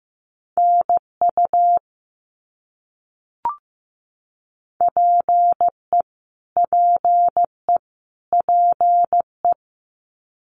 Morse code, heard as a keyed tone, about 15 wpm, 700 Hz. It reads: NU PE PE PE
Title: Canadian Provinces and Territories - Repeated 3x - 15wpm